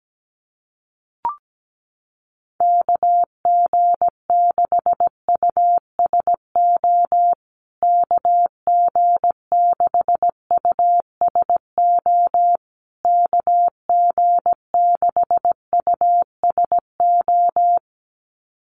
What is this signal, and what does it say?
Morse code, heard as a keyed tone, about 17 wpm, 700 Hz. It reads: KG6USO KG6USO KG6USO
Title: USA Call Signs Repeated 3x 17wpm